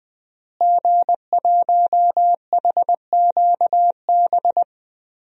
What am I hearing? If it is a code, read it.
Morse code, heard as a keyed tone, about 20 wpm, 700 Hz. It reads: G1HQB
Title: DXCC-100 Call Signs - 5 Characters - Repeated 3x - 20wpm